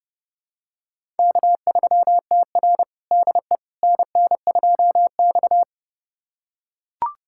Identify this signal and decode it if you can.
Morse code, heard as a keyed tone, about 30 wpm, 700 Hz. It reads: K3TR DE NN2X
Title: USA Callsign DE Callsign - 4 Characters - 30wpm